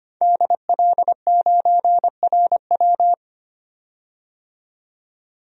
Morse code, heard as a keyed tone, about 25 wpm, 700 Hz. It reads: DL9RW
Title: DXCC-100 Call Signs - 5 Characters - 25wpm